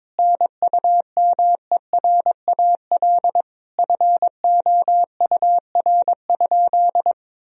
Morse code, heard as a keyed tone, about 22 wpm, 700 Hz. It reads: NUMERAL FOUR?
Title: Sentences from Top 1000 Words - Review Part 2 - 22wpm